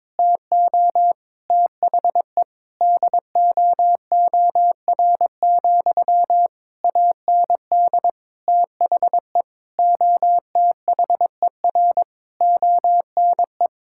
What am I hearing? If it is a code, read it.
Morse code, heard as a keyed tone, about 22 wpm, 700 Hz. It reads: TO THE DOOR, AND THE OTHER ONE
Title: Sentences from Top 1600 Words 22wpm